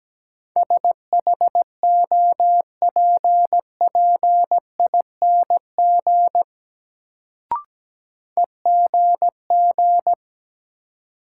Morse code, heard as a keyed tone, about 17 wpm, 700 Hz. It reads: SHOPPING EGG
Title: Sentences from Top 1000 Words - Review Part 7 - New Word Drill - 17wpm